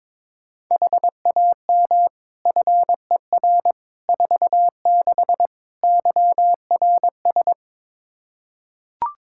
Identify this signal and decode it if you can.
Morse code, heard as a keyed tone, about 22 wpm, 700 Hz. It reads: HAM FER 46 YRS
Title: Practice QSO - Exchange 3 - More Info - 22wpm